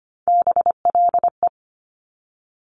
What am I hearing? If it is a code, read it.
Morse code, heard as a keyed tone, about 25 wpm, 700 Hz. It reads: BLE
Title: Top Three Letter Combinations in Words 25wpm